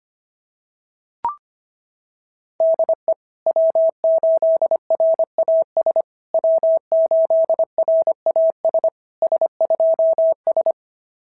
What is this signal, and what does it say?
Morse code, heard as a keyed tone, about 25 wpm, 690 Hz. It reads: DE W8RAS W8RAS S2S
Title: SOTA QSO Practice 25wpm